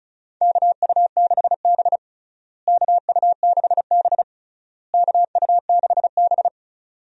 Morse code, heard as a keyed tone, about 34 wpm, 700 Hz. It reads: KU6B KU6B KU6B
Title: USA Call Signs - 4 Characters - Repeated 3x - 35wpm